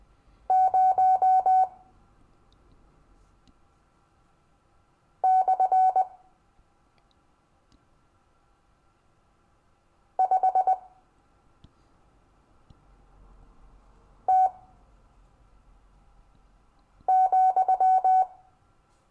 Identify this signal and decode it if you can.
Morse code, heard as a keyed tone, about 20 wpm, spaced as for 3 wpm, 750 Hz. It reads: 0 / 5 T,
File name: cw / Lesson15 five forwardslash